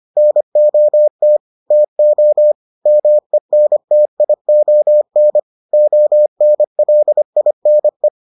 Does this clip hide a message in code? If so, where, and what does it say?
Yes: Morse code from 0.2 to 8.1 s, heard as a keyed tone, about 25 wpm, 600 Hz. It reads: NOT TO MENTION ONLINE